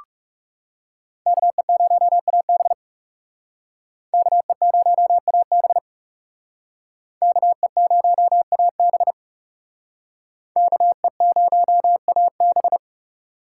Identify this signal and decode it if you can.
Morse code, heard as a keyed tone, about 38 wpm, 700 Hz. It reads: KE0AB KE0AB KE0AB KE0AB